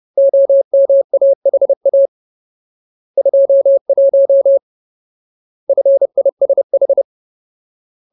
Morse code, heard as a keyed tone, about 30 wpm, 550 Hz. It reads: OMAHA 21 FISH
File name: Ultimate Word List / 30WPM Ultimate